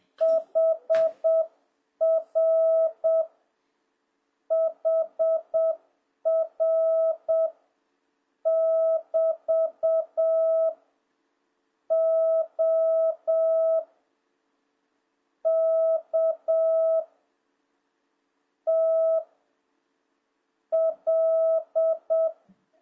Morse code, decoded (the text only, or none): HR HR <BT> O K T L